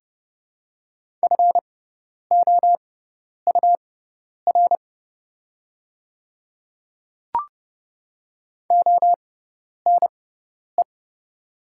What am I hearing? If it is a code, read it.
Morse code, heard as a keyed tone, about 30 wpm, spaced as for 10 wpm, 700 Hz. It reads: FOUR ONE
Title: 6x Character Spacing - Top 300 Words - 30wpm